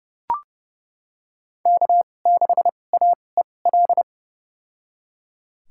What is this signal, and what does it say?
Morse code, heard as a keyed tone, about 30 wpm, spaced as for 22 wpm, 700 Hz. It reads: K6AEL
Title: USA Call Signs - 5 Characters - 2x Character Spacing - 30wpm